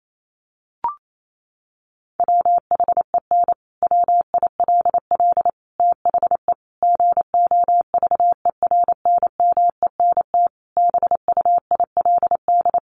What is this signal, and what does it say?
Morse code, heard as a keyed tone, about 28 wpm, 700 Hz. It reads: WHEN WILL THE GOVERNMENT BUILD
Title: Sentences from Top 1400 Words 28wpm